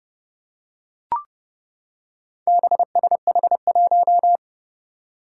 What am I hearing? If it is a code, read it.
Morse code, heard as a keyed tone, about 30 wpm, 700 Hz. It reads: BSH1
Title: Four Challenging Letter-Number Combinations 30wpm